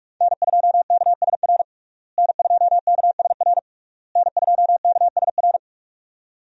Morse code, heard as a keyed tone, about 47 wpm, 700 Hz. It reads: N2KSR N2KSR N2KSR
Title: USA Call Signs - 5 Characters - Repeated 3x - 45wpm